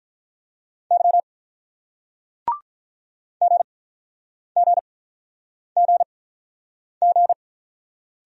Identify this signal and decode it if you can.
Morse code, heard as a keyed tone, about 46 wpm, spaced as for 9 wpm, 700 Hz. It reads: Y GGGG